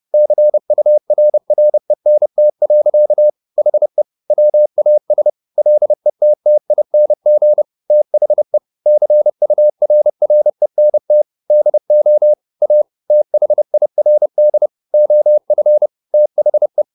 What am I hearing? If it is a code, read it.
Morse code, heard as a keyed tone, about 30 wpm, 600 Hz. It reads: CURRENT. HE WAS LETTING THE CURRENT DO A THIRD OF THE
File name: the old man and the sea / the old man and the sea 1 30wpm